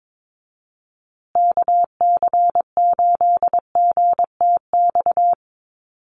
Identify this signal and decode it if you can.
Morse code, heard as a keyed tone, about 22 wpm, 700 Hz. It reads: KC8GTX
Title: USA Call Signs Repeated 2x 22wpm